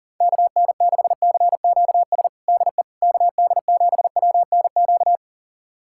Morse code, heard as a keyed tone, about 40 wpm, 700 Hz. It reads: KN6CQS DE KD7WNQ
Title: USA Callsign DE Callsign - 6 Characters - 40wpm